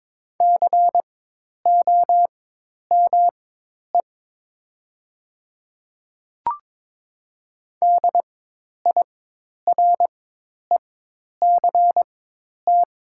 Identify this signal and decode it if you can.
Morse code, heard as a keyed tone, about 22 wpm, spaced as for 10 wpm, 700 Hz. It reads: COME DIRECT
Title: 4x Character Spacing - Top 500 Words - 22wpm